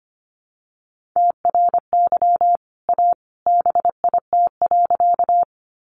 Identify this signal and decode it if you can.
Morse code, heard as a keyed tone, about 25 wpm, 700 Hz. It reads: TRY A BIT.
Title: Sentences from Top 900 Words 25wpm